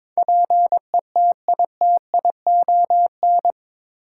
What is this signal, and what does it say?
Morse code, heard as a keyed tone, about 22 wpm, 700 Hz. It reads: PETITION
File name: practice / 22wpm / Eight Letter Words - 22wpm